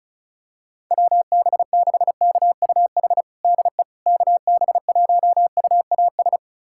Morse code, heard as a keyed tone, about 36 wpm, 700 Hz. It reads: WB6KUH DE KB1UAS
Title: USA Callsign DE Callsign - 6 Characters - 35wpm